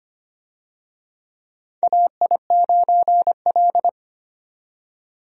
Morse code, heard as a keyed tone, about 25 wpm, 700 Hz. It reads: AI9L
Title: USA Call Signs - 4 Characters - 25wpm